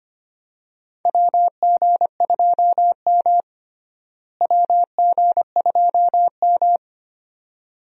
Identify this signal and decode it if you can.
Morse code, heard as a keyed tone, about 25 wpm, 700 Hz. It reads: WG2M WG2M
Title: USA Call Signs - 4 Characters - Repeated 2x - 25wpm